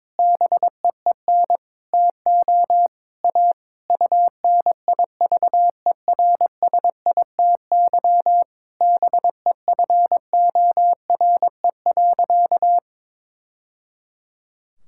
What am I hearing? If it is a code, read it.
Morse code, heard as a keyed tone, about 22 wpm, 700 Hz. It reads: BEEN TO A UNIVERSITY BEFORE.
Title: Sentences from Top 1000 Words - Review Part 5 - 22wpm